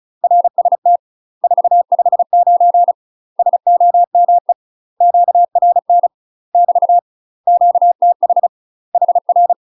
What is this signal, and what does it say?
Morse code, heard as a keyed tone, about 36 wpm, 700 Hz. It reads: RST 459 SOME QRN <BT> QTH HR
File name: UK MASTER QSOs TRAINING / 35WPM / UK MASTER QSOs 0009